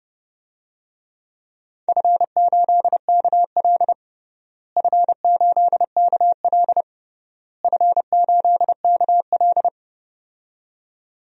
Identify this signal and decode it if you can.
Morse code, heard as a keyed tone, about 30 wpm, 700 Hz. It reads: F8KL F8KL F8KL
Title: DXCC-100 Call Signs - 4 Characters - Repeated 3x - 30wpm